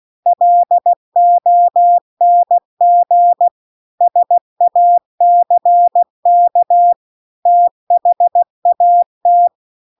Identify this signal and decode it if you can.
Morse code, heard as a keyed tone, about 16 wpm, 700 Hz. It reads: LONG SACK THAT
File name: CW-The-Big-Twoheaded-River-2-16WPM